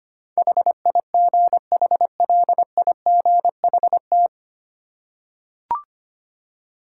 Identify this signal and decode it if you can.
Morse code, heard as a keyed tone, about 25 wpm, 700 Hz. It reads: HIGHLIGHT
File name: practice / 25wpm / 9 Letter Words - First 3 Letters Given - 25wpm